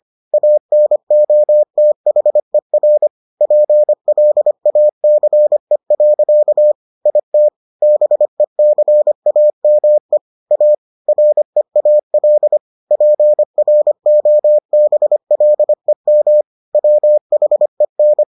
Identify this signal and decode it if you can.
Morse code, heard as a keyed tone, about 25 wpm, 600 Hz. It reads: ANOTHER PLACE. IT BECAME A REAL PROBLEM WHEN